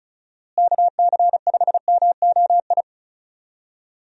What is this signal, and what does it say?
Morse code, heard as a keyed tone, about 36 wpm, 700 Hz. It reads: KC5MOI
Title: USA Call Signs - 6 Characters - Repeated 3x - 35wpm